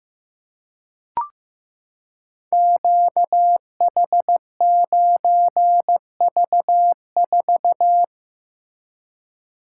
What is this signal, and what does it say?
Morse code, heard as a keyed tone, about 15 wpm, 700 Hz. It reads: QH9V4
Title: Code Groups - 5 Letter:Numbers - 15wpm